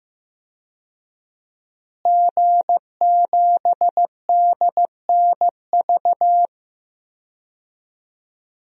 Morse code, heard as a keyed tone, about 15 wpm, 700 Hz. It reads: G7DNV
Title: DXCC-100 Call Signs - 5 Characters - Repeated 3x - 15wpm